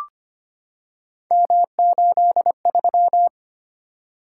Morse code, heard as a keyed tone, about 25 wpm, 700 Hz. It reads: M83
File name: Three Letter-Number Combinations - 25wpm